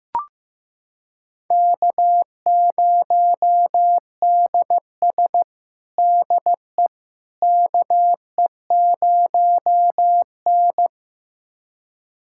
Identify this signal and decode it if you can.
Morse code, heard as a keyed tone, about 15 wpm, 700 Hz. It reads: K0DS DE KE0N
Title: USA Callsign DE Callsign - 4 Characters - 15wpm